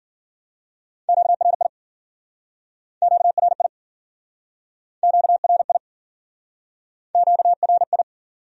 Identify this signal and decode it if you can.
Morse code, heard as a keyed tone, about 57 wpm, 700 Hz. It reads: QRI QRI QRI QRI